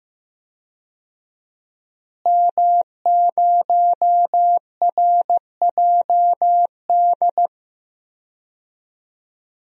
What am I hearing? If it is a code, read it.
Morse code, heard as a keyed tone, about 15 wpm, 700 Hz. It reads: M0RJD
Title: DXCC-100 Call Signs - 5 Characters - Repeated 3x - 15wpm